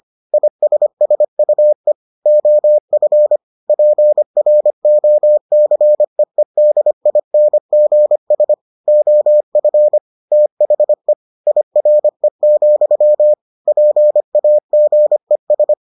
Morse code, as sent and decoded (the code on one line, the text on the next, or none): .. ... ... ..- . / --- ..-. / .--. .-. --- -.-. . -.. .. -. --. ... / --- ..-. / - .... / .. .-. --..-- / .--. .- --. ...
ISSUE OF PROCEEDINGS OF THE IRE, PAGES